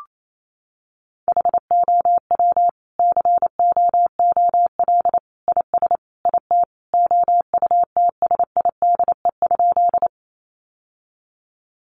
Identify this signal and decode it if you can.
Morse code, heard as a keyed tone, about 28 wpm, 700 Hz. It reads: HOW COOL IS IT OUTSIDE?
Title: Sentences from Top 1200 Words 28wpm